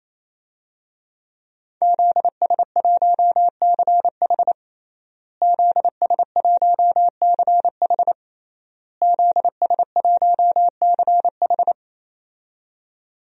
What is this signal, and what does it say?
Morse code, heard as a keyed tone, about 27 wpm, 700 Hz. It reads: ZS1CH ZS1CH ZS1CH
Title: DXCC-100 Call Signs - 5 Characters - Repeated 3x - 28wpm